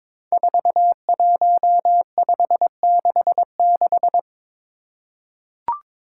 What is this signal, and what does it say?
Morse code, heard as a keyed tone, about 22 wpm, 700 Hz. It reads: 41566
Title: Five Challenging Letter-Number Combinations 22wpm